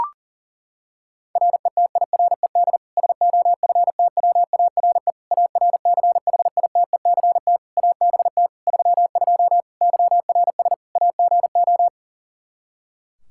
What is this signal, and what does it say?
Morse code, heard as a keyed tone, about 40 wpm, 700 Hz. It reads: RETIRED SOFTWARE ARCHITECT ABT 32 YRS AGO
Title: Practice QSO - Exchange 3 - More Info - 40wpm